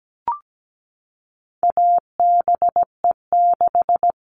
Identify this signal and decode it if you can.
Morse code, heard as a keyed tone, about 17 wpm, 700 Hz. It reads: ABE6
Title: All Cut Numbers - 4-Digit 17wpm